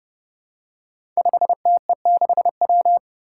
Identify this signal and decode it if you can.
Morse code, heard as a keyed tone, about 30 wpm, 700 Hz. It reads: 5TE6W
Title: Five Challenging Letter-Number Combinations 30wpm